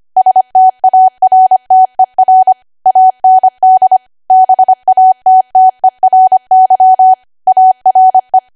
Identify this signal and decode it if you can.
Morse code, heard as a keyed tone, about 25 wpm, 750 Hz. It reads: STARTER AND BATTERY ARE
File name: Archive / 25 WPM / 25WPM